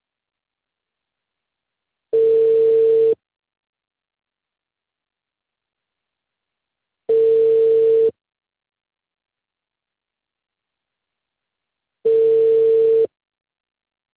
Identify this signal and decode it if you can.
Ringback tone (China, 450 Hz).